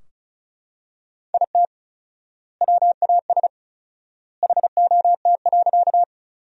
Morse code, heard as a keyed tone, about 36 wpm, 700 Hz. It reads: IT WAS HOT.
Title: 4x Word-Spacing - Sentences from Top 500 Words - 35wpm